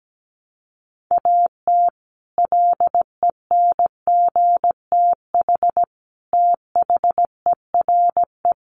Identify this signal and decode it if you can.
Morse code, heard as a keyed tone, about 17 wpm, 700 Hz. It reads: AT LENGTH THERE
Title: Sentences from Top 1000 Words 17wpm